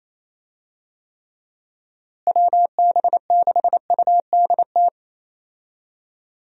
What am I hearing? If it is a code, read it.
Morse code, heard as a keyed tone, about 28 wpm, 700 Hz. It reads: WB6UDT
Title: USA Call Signs - 6 Characters - Repeated 3x - 28wpm